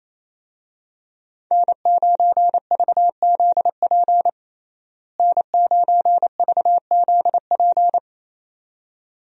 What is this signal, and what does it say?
Morse code, heard as a keyed tone, about 28 wpm, 700 Hz. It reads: N9VZP N9VZP
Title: USA Call Signs - 5 Characters - Repeated 2x - 28wpm